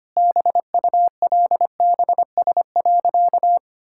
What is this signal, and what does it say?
Morse code, heard as a keyed tone, about 25 wpm, 700 Hz. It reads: BULBS.
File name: Easy Sentences with Hints - 25wpm